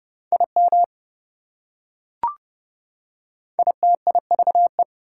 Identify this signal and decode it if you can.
Morse code, heard as a keyed tone, about 29 wpm, 700 Hz. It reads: IM ITIVE